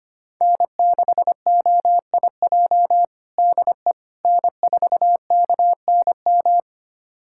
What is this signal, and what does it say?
Morse code, heard as a keyed tone, about 25 wpm, 700 Hz. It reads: N6OIJ DE N4KNM
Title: USA Callsign DE Callsign - 5 Characters - 25wpm